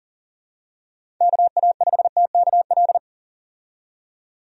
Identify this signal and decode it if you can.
Morse code, heard as a keyed tone, about 40 wpm, 700 Hz. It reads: KA5TKL
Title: USA Call Signs - 6 Characters - 40wpm